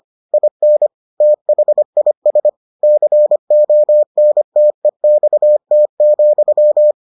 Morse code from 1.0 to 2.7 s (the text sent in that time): THIS